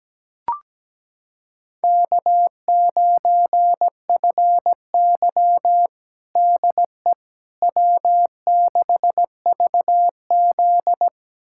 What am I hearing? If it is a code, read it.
Morse code, heard as a keyed tone, about 17 wpm, 700 Hz. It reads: K9FY DE W6VZ
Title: USA Callsign DE Callsign - 4 Characters - 17wpm